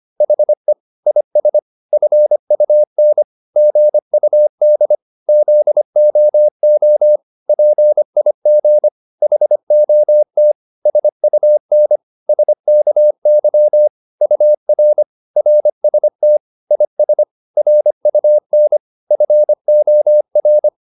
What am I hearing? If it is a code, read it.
Morse code, heard as a keyed tone, about 25 wpm, 600 Hz. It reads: HE IS FUN GUD ZOO PIG HOT SUN SKY UR RST IS RUN FOR